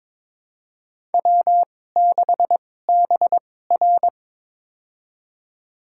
Morse code, heard as a keyed tone, about 22 wpm, spaced as for 16 wpm, 700 Hz. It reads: W6BR
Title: USA Call Signs - 2x Character Spacing - 22wpm